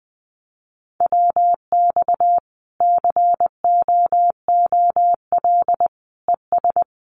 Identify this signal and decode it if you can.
Morse code, heard as a keyed tone, about 20 wpm, 700 Hz. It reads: WX COOL ES
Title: Common QSO Phrases 20wpm